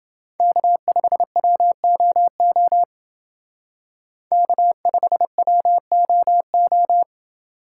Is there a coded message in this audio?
Yes: Morse code from 0.4 to 7.0 s, heard as a keyed tone, about 30 wpm, 700 Hz. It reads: K5WOO K5WOO